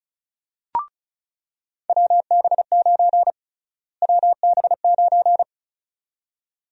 Morse code, heard as a keyed tone, about 36 wpm, 700 Hz. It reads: WB9 WB9
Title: USA Call Sign Prefixes - Repeated 2x - 35wpm